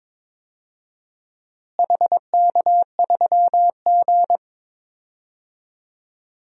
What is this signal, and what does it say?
Morse code, heard as a keyed tone, about 22 wpm, 700 Hz. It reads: HK3G
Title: DXCC-100 Call Signs - 4 Characters - Repeated 3x - 22wpm